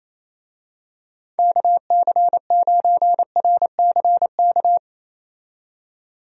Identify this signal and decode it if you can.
Morse code, heard as a keyed tone, about 27 wpm, 700 Hz. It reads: KC9RCK
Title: USA Call Signs - 6 Characters - 28wpm